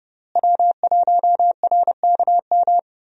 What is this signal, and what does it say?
Morse code, heard as a keyed tone, about 30 wpm, 700 Hz. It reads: W1RKM
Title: USA Call Signs - 5 Characters - Repeated 2x - 30wpm